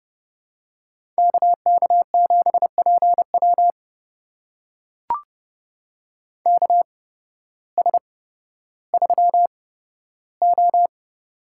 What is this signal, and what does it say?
Morse code, heard as a keyed tone, about 30 wpm, spaced as for 14 wpm, 700 Hz. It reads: KK7PW KS3O
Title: USA Call Signs - 8x Character Spacing - 30wpm